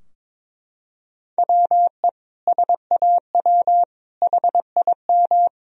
Morse code, heard as a keyed tone, about 21 wpm, 700 Hz. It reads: WE SAW HIM
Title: Sentences from Top 1000 Words - Review Part 4 - 22wpm